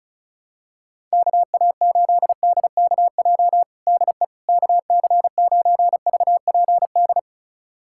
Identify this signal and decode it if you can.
Morse code, heard as a keyed tone, about 36 wpm, 700 Hz. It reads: KA8DKJ DE KC9VPD
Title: USA Callsign DE Callsign - 6 Characters - 35wpm